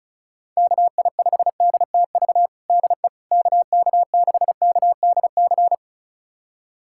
Morse code, heard as a keyed tone, about 34 wpm, 700 Hz. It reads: KI5DTV DE KK6KDC
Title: USA Callsign DE Callsign - 6 Characters - 35wpm